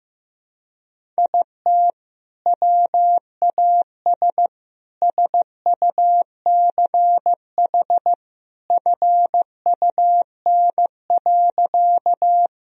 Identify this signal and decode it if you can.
Morse code, heard as a keyed tone, about 15 wpm, 700 Hz. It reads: IT WAS SUCH FUN.
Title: Sentences from Top 1000 Words - Review Part 4 - 15wpm